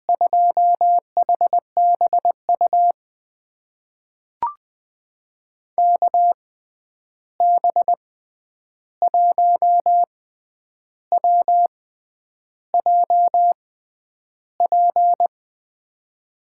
Morse code, heard as a keyed tone, about 20 wpm, spaced as for 9 wpm, 700 Hz. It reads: 2HBU KB1WJP
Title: USA Call Signs - 6 Characters - 6x Character Spacing - 20wpm